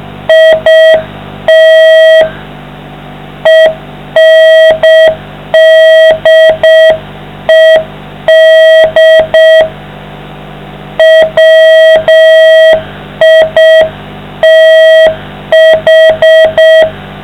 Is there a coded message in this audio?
Yes: Morse code from 0.3 to 16.8 s, heard as a keyed tone, about 6 wpm, 660 Hz, with noise. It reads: IT ENDED WITH